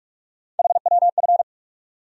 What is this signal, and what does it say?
Morse code, heard as a keyed tone, about 46 wpm, 700 Hz. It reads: HWF
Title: USA Call Sign Postfixes - 3 Letters 45wpm